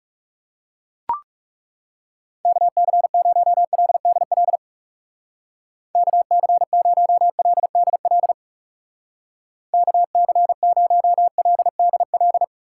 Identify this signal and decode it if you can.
Morse code, heard as a keyed tone, about 36 wpm, 700 Hz. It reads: KC0LDL KC0LDL KC0LDL
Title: Speed Racing - USA Call Signs 30wpm